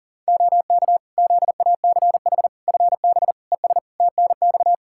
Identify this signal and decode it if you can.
Morse code, heard as a keyed tone, about 40 wpm, 700 Hz. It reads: OK ZACH FB ES TNX
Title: Practice QSO - Exchange 2 - Info - 40wpm